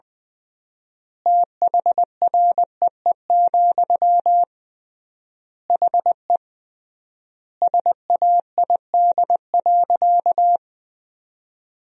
Morse code, heard as a keyed tone, about 20 wpm, 700 Hz. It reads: THREE, HE SAID.
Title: 3x Word-Spacing - Sentences from Top 200 Words - 20wpm